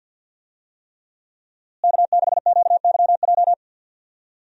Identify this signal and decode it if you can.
Morse code, heard as a keyed tone, about 51 wpm, 690 Hz. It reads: K6QYJ
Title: USA Call Signs - 5 Characters - Repeated 3x - 50wpm